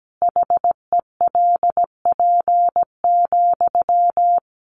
Morse code, heard as a keyed tone, about 17 wpm, 700 Hz. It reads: HELP,
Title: Sentences from Top 300 Words 17wpm